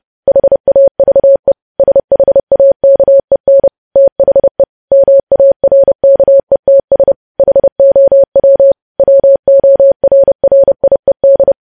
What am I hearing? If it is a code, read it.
Morse code, heard as a keyed tone, about 30 wpm, 550 Hz. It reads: HAVE SHAKEN THE MARKETS HOW WORRIED
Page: news headlines in morse code at 30 wpm